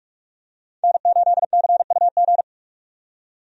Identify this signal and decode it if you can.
Morse code, heard as a keyed tone, about 45 wpm, 700 Hz. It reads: N8CUG
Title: USA Call Signs - 5 Characters - 45wpm